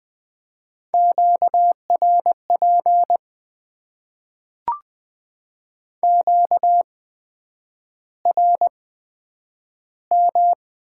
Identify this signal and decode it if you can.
Morse code, heard as a keyed tone, about 20 wpm, spaced as for 9 wpm, 700 Hz. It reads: QRP QRM
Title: Top 90 QSO Elements - 8x Character Spacing - 20wpm